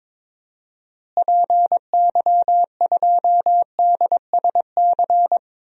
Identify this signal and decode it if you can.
Morse code, heard as a keyed tone, about 22 wpm, 700 Hz. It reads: PY2DSC
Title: DXCC-100 Call Signs - 6 Characters - Repeated 2x - 22wpm